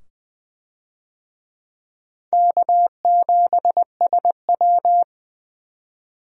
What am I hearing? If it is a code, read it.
Morse code, heard as a keyed tone, about 21 wpm, 700 Hz. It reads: K7SW
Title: USA Call Signs - 4 Characters - 20wpm